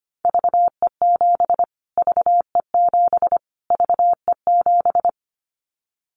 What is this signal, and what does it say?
Morse code, heard as a keyed tone, about 25 wpm, 700 Hz. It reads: VE7 VE7 VE7
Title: Callsign Prefixes for Top 25 DXCC - Repeated 3x - 25wpm